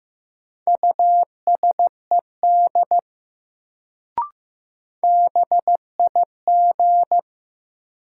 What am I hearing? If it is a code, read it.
Morse code, heard as a keyed tone, about 15 wpm, 700 Hz. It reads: USED BIG
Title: Sets of 2 Words - New Word Drill - 15wpm